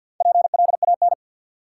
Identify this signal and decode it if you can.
Morse code, heard as a keyed tone, about 48 wpm, 700 Hz. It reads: PLAN
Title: Sentences from Top 500 Words - New Word Drill - 50wpm